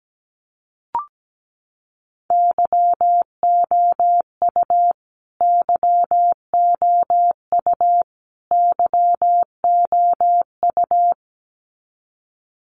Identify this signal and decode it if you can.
Morse code, heard as a keyed tone, about 17 wpm, 700 Hz. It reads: YOU YOU YOU
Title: Top 300 Words Repeated 17wpm